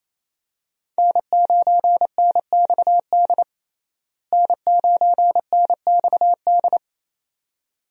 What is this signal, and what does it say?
Morse code, heard as a keyed tone, about 28 wpm, 700 Hz. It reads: N9NXD N9NXD
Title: USA Call Signs - 5 Characters - Repeated 2x - 28wpm